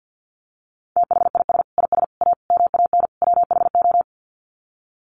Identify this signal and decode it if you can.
Morse code, heard as a keyed tone, about 50 wpm, 700 Hz. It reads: THIS IS A MAN WHO
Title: Sets of 5 Words Encore2 50wpm